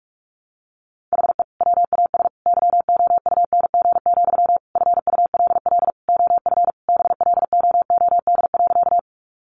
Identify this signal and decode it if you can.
Morse code, heard as a keyed tone, about 44 wpm, 700 Hz. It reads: HE WAS YOUNG, FULL OF BLOOD.
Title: Sentences from Top 1100 Words 45wpm